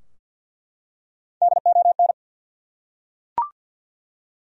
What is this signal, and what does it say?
Morse code, heard as a keyed tone, about 51 wpm, 700 Hz. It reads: DON
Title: USA Common Names - Max 5 Letters - 50wpm